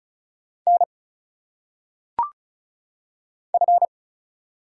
Morse code, heard as a keyed tone, about 36 wpm, 700 Hz. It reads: N F
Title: Single Letters - Phonetic Pronunciation 35wpm